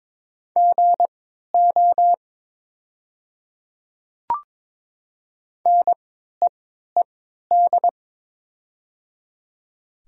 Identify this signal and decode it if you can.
Morse code, heard as a keyed tone, about 22 wpm, spaced as for 13 wpm, 700 Hz. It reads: GO NEED
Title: 3x Character Spacing - Top 200 Words - 22wpm